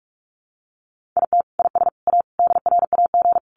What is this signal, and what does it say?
Morse code, heard as a keyed tone, about 44 wpm, 700 Hz. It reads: IT IS A DRAG
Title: Sentences from Top 1500 Words 45wpm